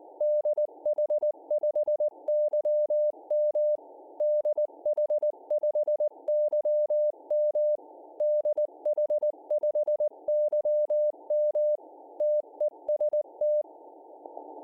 Morse code, heard as a keyed tone, about 19 wpm, 600 Hz. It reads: DH5YM DH5YM DH5YM TEST